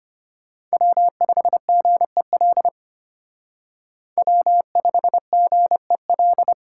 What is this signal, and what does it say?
Morse code, heard as a keyed tone, about 33 wpm, 700 Hz. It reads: W5GEL W5GEL